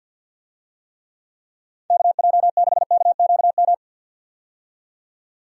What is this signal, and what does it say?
Morse code, heard as a keyed tone, about 51 wpm, 700 Hz. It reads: KJ6KQM
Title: USA Call Signs - 6 Characters - 50wpm